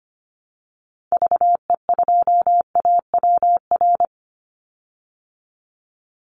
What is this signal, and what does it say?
Morse code, heard as a keyed tone, about 25 wpm, 700 Hz. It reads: VE2AWR
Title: Call Signs Repeated 3x - 25wpm